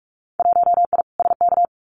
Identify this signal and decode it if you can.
Morse code, heard as a keyed tone, about 46 wpm, 690 Hz. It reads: 1I SK
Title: Winter Field Day Exchanges Encore 45wpm